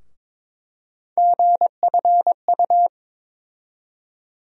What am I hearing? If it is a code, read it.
Morse code, heard as a keyed tone, about 21 wpm, 700 Hz. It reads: GFU
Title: USA Call Sign Postfixes - 3 Letters 22wpm